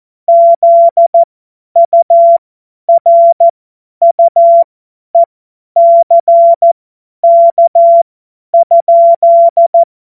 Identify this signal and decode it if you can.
Morse code, heard as a keyed tone, about 14 wpm, spaced as for 10 wpm, 680 Hz. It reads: ZURUECK?